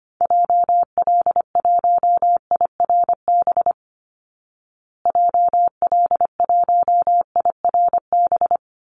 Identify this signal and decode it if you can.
Morse code, heard as a keyed tone, about 25 wpm, 700 Hz. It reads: JL1IRB JL1IRB
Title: Call Signs Repeated 3x - 25wpm